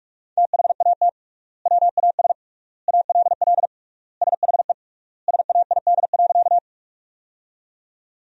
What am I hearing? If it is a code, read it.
Morse code, heard as a keyed tone, about 45 wpm, 700 Hz. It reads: THAT WAS ALL SHE SAID.
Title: 3x Word-Spacing - Sentences from Top 100 Words - 45wpm